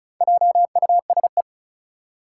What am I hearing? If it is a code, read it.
Morse code, heard as a keyed tone, about 36 wpm, 700 Hz. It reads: JUSE
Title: Four Challenging Letter-Number Combinations 35wpm